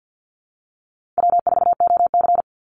Morse code, heard as a keyed tone, about 50 wpm, 700 Hz. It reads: W4OC
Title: Easy Call Signs Repeated 3x - 50wpm